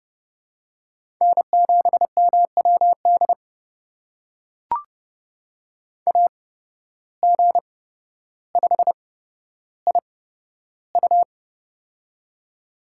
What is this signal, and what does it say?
Morse code, heard as a keyed tone, about 30 wpm, spaced as for 13 wpm, 700 Hz. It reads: N7MWD AG5IU
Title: USA Call Signs - 5 Characters - 8x Character Spacing - 30wpm